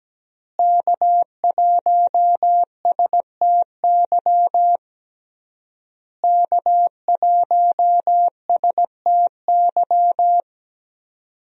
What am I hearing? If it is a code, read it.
Morse code, heard as a keyed tone, about 17 wpm, 700 Hz. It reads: K1STY K1STY
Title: USA Call Signs - 5 Characters - Repeated 3x - 17wpm